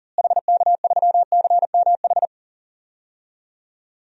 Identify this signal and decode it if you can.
Morse code, heard as a keyed tone, about 40 wpm, 700 Hz. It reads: HK3CMH
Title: DXCC-100 Call Signs - 6 Characters - Repeated 3x - 40wpm